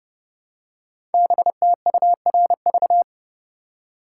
Morse code, heard as a keyed tone, about 30 wpm, 700 Hz. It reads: BTURV
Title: Code Groups - 5 Letters - 30wpm